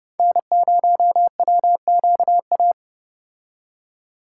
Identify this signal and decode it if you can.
Morse code, heard as a keyed tone, about 30 wpm, 700 Hz. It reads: N0WQA